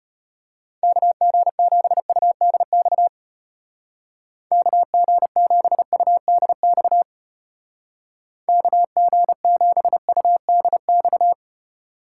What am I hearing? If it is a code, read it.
Morse code, heard as a keyed tone, about 31 wpm, 700 Hz. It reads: KG7UDX KG7UDX KG7UDX
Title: Speed Racing - USA 2x3 Call Signs 25wpm